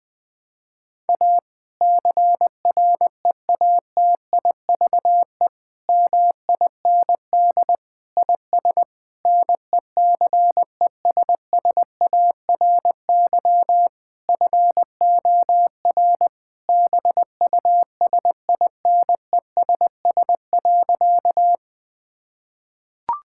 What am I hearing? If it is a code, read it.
Morse code, heard as a keyed tone, about 21 wpm, 700 Hz. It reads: A CREATIVE MIND IS NECESSARY FOR BUSINESS.
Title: Sentences from Top 1000 Words - Review Part 5 - 20wpm